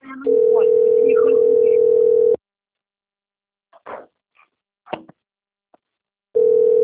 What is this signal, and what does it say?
Ringback tone (North America), 440 + 480 Hz.